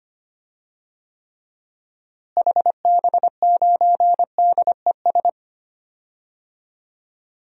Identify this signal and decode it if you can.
Morse code, heard as a keyed tone, about 25 wpm, 700 Hz. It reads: HB9DES
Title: DXCC-100 Call Signs - 6 Characters - Repeated 2x - 25wpm